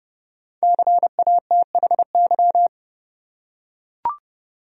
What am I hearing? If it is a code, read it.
Morse code, heard as a keyed tone, about 30 wpm, 700 Hz. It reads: CATHY
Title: USA Common Names - 30wpm